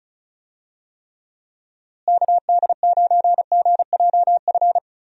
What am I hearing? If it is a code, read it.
Morse code, heard as a keyed tone, about 36 wpm, 700 Hz. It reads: KD9GJF
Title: USA Call Signs - 6 Characters - Repeated 2x - 35wpm